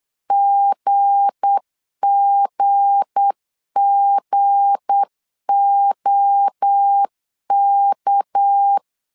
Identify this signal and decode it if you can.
Morse code, heard as a keyed tone, about 8 wpm, 800 Hz. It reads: GGGOK
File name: LEZIONI / 7 GF9Z / 7 G 8wpm 3118